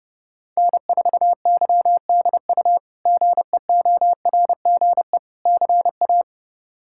Morse code, heard as a keyed tone, about 32 wpm, 700 Hz. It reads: N4YDU GEORGE CA
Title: K1USN SST QSO Practice - 30wpm